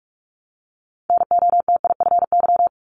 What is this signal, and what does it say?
Morse code, heard as a keyed tone, about 46 wpm, 700 Hz. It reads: NOTIFY
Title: Multi-letter Phonemes Repeated 3x 45wpm